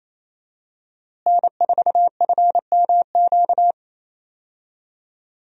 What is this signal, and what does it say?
Morse code, heard as a keyed tone, about 27 wpm, 700 Hz. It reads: N4FMQ
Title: USA Call Signs - 5 Characters - Repeated 3x - 28wpm